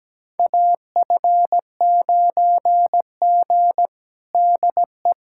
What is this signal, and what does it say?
Morse code, heard as a keyed tone, about 17 wpm, 700 Hz. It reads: AF9G DE
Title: USA Callsign DE Callsign - 4 Characters - 17wpm